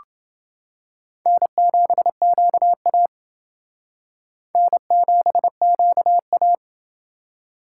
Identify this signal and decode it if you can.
Morse code, heard as a keyed tone, about 28 wpm, 700 Hz. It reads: N7QA N7QA